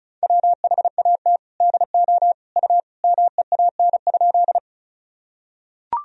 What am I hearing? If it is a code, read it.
Morse code, heard as a keyed tone, about 36 wpm, 700 Hz. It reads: WHAT DO U MEAN?
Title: Sentences from Top 200 Words and Abbreviations 35wpm